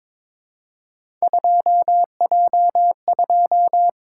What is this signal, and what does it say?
Morse code, heard as a keyed tone, about 22 wpm, 700 Hz. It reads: 2J2
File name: practice / 22wpm / Three Letter-Number Combinations - 22wpm